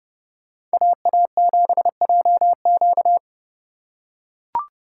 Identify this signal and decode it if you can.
Morse code, heard as a keyed tone, about 29 wpm, 700 Hz. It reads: AA7JQ